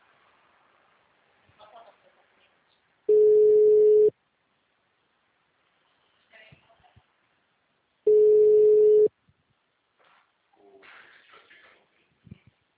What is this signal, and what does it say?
Ringback tone (Europe, 425 Hz), with speech.
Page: Itu - SP